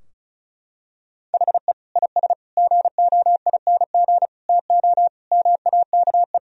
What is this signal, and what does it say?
Morse code, heard as a keyed tone, about 36 wpm, 700 Hz. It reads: HE IS GOING TO MAKE
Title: Sentences from Top 1000 Words - Review Part 4 - 35wpm